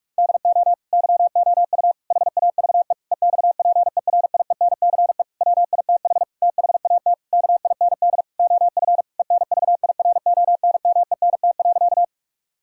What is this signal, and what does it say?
Morse code, heard as a keyed tone, about 46 wpm, 700 Hz. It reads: DO YOU HAVE EXPERIENCE WITH THAT KIND OF ENVIRONMENT.
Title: Sentences from Top 1000 Words - Review Part 5 - 45wpm